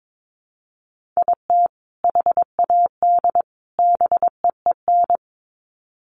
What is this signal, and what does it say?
Morse code, heard as a keyed tone, about 22 wpm, 700 Hz. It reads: IT HAD BEEN
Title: Sets of 3 Words 22wpm